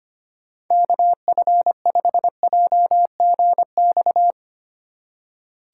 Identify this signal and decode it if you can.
Morse code, heard as a keyed tone, about 25 wpm, 700 Hz. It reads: KF5JGX
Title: USA Call Signs - 6 Characters - 3x Character Spacing - 25wpm